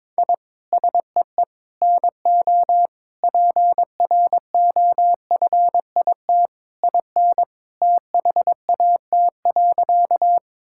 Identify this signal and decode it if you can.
Morse code, heard as a keyed tone, about 22 wpm, 700 Hz. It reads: I SEE NO PROFIT IN THAT.
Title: Sentences from Top 1000 Words - Review Part 5 - 22wpm